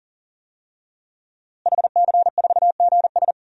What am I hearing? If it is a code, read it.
Morse code, heard as a keyed tone, about 37 wpm, 700 Hz. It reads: HC4GS
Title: DXCC-100 Call Signs - 5 Characters - Repeated 3x - 40wpm